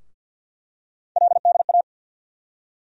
Morse code, heard as a keyed tone, about 47 wpm, 700 Hz. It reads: LDA
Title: USA Call Sign Postfixes - 3 Letters 50wpm